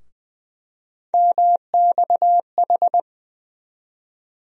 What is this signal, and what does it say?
Morse code, heard as a keyed tone, about 20 wpm, 700 Hz. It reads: MXH
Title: USA Call Sign Postfixes - 3 Letters 20wpm